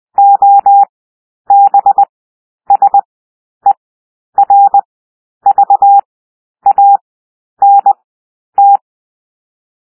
Morse code, heard as a keyed tone, about 20 wpm, 800 Hz. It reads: O B S E R V A N T
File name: cwmp3 / W1 SOUND WORDS 20 wpm